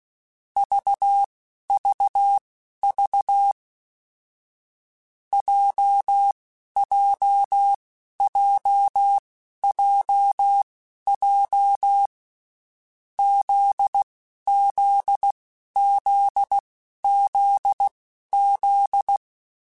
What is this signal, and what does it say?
Morse code, heard as a keyed tone, about 16 wpm, spaced as for 11 wpm, 800 Hz. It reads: VVV JJJJJ ZZZZZ
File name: Basic Sets / Basic Letter Groups Collective7 / JZ17@10